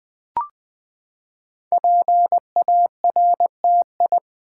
Morse code, heard as a keyed tone, about 20 wpm, 700 Hz. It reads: PARTI
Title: Build A Word - 5 Letters At A Time - 12-Letter Words - 20wpm